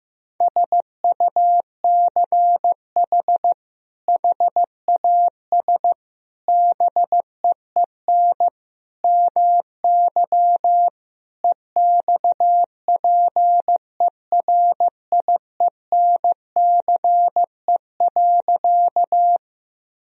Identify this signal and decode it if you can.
Morse code, heard as a keyed tone, about 15 wpm, 700 Hz. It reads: SUCH HAS BEEN MY EXPERIENCE.
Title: Sentences from Top 1000 Words - Review Part 3 - 15wpm